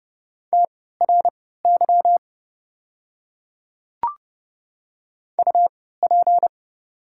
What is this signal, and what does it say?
Morse code, heard as a keyed tone, about 30 wpm, spaced as for 17 wpm, 700 Hz. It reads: TRY UP
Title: 3x Character Spacing - Top 300 Words - 30wpm